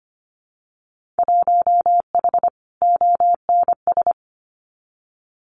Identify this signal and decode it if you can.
Morse code, heard as a keyed tone, about 25 wpm, 700 Hz. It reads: 1H ONS
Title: Winter Field Day Exchanges Encore 25wpm